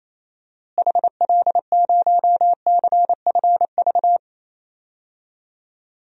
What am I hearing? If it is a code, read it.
Morse code, heard as a keyed tone, about 28 wpm, 700 Hz. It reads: HL0CFV
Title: DXCC-100 Call Signs - 6 Characters - Repeated 2x - 28wpm